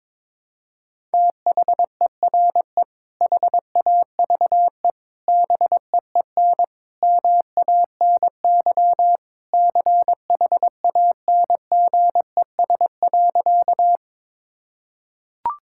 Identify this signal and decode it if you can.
Morse code, heard as a keyed tone, about 22 wpm, 700 Hz. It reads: THERE HAVE BEEN MANY CHANGES.
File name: practice / 22wpm / Sentences with 3-Word Phrases - 22wpm